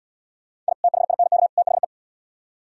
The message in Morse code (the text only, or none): I THINK THE